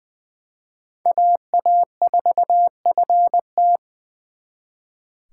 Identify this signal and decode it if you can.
Morse code, heard as a keyed tone, about 24 wpm, 700 Hz. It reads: AA4FT
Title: Speed Racing - USA 2x2 Call Signs 20wpm